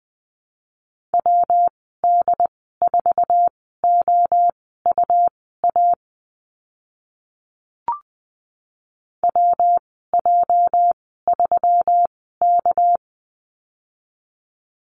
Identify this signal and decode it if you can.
Morse code, heard as a keyed tone, about 20 wpm, spaced as for 14 wpm, 700 Hz. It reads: WD4OUA WJ3K
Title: 207 Lesson 33g - Letter j - Call Signs 20wpm